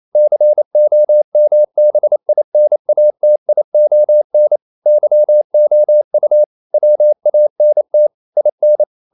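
Morse code, heard as a keyed tone, about 28 wpm, 600 Hz. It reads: COMBINATION YOU WANT IN
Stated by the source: Morse code practice at 28 wpm, text of Ken Follett's novel Never in English, from CW med Gnister.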